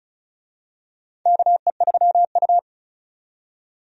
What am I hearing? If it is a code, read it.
Morse code, heard as a keyed tone, about 34 wpm, 700 Hz. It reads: KE3U
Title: USA Call Signs - 4 Characters - 35wpm